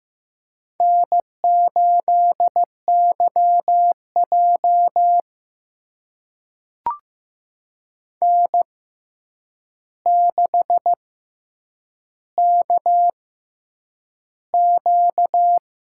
Morse code, heard as a keyed tone, about 15 wpm, spaced as for 8 wpm, 700 Hz. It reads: N8YJ N6KQ
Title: USA Call Signs - 4 Characters - 6x Character Spacing - 15wpm